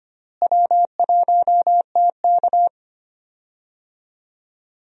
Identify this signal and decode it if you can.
Morse code, heard as a keyed tone, about 25 wpm, 700 Hz. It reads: W1TK
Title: USA Call Signs - 4 Characters - Repeated 2x - 25wpm